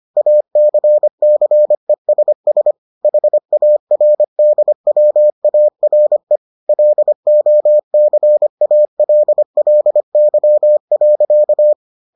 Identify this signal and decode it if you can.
Morse code, heard as a keyed tone, about 25 wpm, 600 Hz. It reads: ACCESS HARDWARE LOCALLY.